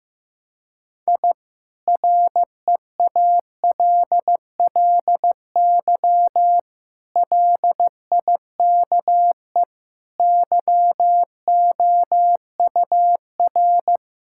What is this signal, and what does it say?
Morse code, heard as a keyed tone, about 16 wpm, 700 Hz. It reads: I REALLY LIKE YOUR
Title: Sentences from Top 1000 Words - Review Part 6 - 15wpm